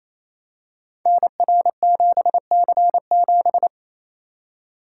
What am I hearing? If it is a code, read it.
Morse code, heard as a keyed tone, about 27 wpm, 700 Hz. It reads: NR7C7
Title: Code Groups - 5 Letter:Numbers - 28wpm